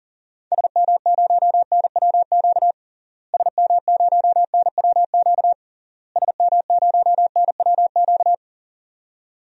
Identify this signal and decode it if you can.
Morse code, heard as a keyed tone, about 40 wpm, 700 Hz. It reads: SM0NWQ SM0NWQ SM0NWQ
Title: DXCC-100 Call Signs - 6 Characters - Repeated 3x - 40wpm